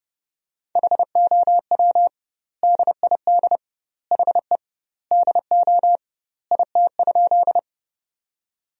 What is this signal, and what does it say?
Morse code, heard as a keyed tone, about 30 wpm, 700 Hz. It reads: HOW DID HE DO IT?
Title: 2x Word-Spacing - Sentences from Top 100 Words - 30wpm